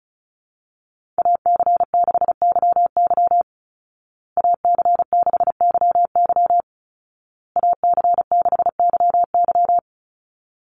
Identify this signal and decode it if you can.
Morse code, heard as a keyed tone, about 34 wpm, 700 Hz. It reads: AC6YY AC6YY AC6YY
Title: Call Signs Repeated 2x - 35wpm